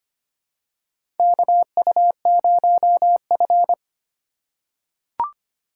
Morse code, heard as a keyed tone, about 25 wpm, 700 Hz. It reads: KU0F
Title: USA Call Signs - 4 Characters - 8x Character Spacing - 25wpm